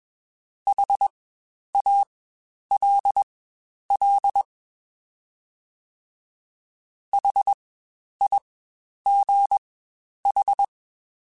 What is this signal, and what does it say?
Morse code, heard as a keyed tone, about 21 wpm, spaced as for 10 wpm, 800 Hz. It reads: HALL HIGH
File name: Spaced Alphas / Spaced Alphas mp3s / Hs sp @20